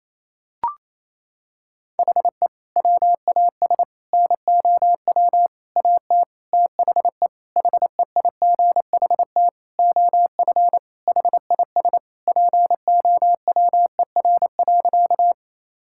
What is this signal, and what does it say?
Morse code, heard as a keyed tone, about 28 wpm, 700 Hz. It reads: HE WAS NOW AT THE HEIGHT OF HIS POWER.
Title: Sentences from Top 1600 Words 28wpm